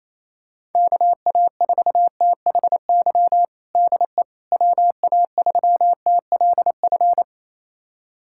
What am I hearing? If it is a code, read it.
Morse code, heard as a keyed tone, about 28 wpm, 700 Hz. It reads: KA4THY DE WA3TLF
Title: USA Callsign DE Callsign 28wpm